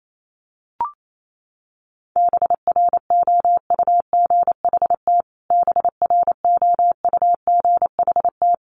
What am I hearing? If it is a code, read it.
Morse code, heard as a keyed tone, about 28 wpm, 700 Hz. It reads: BROUGHT BROUGHT
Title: Top 500 Words Repeated 28wpm